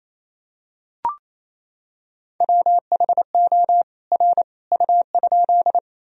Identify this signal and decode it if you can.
Morse code, heard as a keyed tone, about 28 wpm, 700 Hz. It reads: WHO R U?
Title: Sentences from Top 100 Words and Abbreviations 28wpm